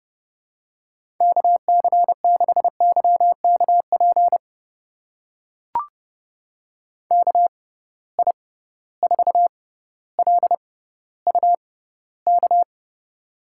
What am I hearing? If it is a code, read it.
Morse code, heard as a keyed tone, about 32 wpm, spaced as for 16 wpm, 700 Hz. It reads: KC6YKP KI4LUK
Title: USA Call Signs - 6 Characters - 6x Character Spacing - 30wpm